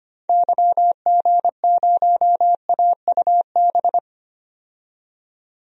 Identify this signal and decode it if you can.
Morse code, heard as a keyed tone, about 25 wpm, 700 Hz. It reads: YG0AUB
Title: DXCC-100 Call Signs - 6 Characters - 25wpm